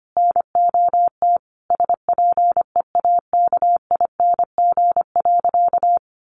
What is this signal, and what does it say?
Morse code, heard as a keyed tone, about 25 wpm, 700 Hz. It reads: NOT SPEAKING.
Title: A Sound Of Thunder 25wpm